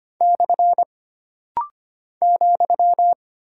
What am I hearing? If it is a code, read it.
Morse code, heard as a keyed tone, about 25 wpm, 700 Hz. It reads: / ,
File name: Single Punctuation-Prosign - Rapid-Fire - 25wpm